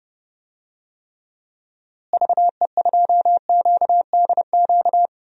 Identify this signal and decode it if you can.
Morse code, heard as a keyed tone, about 29 wpm, 700 Hz. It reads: VE2QDQ
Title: DXCC-100 Call Signs - 6 Characters - Repeated 2x - 30wpm